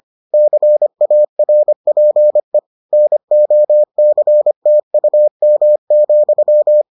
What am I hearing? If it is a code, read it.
Morse code, heard as a keyed tone, about 25 wpm, 600 Hz. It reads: CARPE NOCTUM,